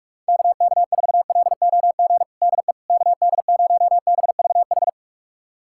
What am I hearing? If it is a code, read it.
Morse code, heard as a keyed tone, about 47 wpm, 700 Hz. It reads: KK4LOG DE KD0BVH
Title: USA Callsign DE Callsign - 6 Characters - 45wpm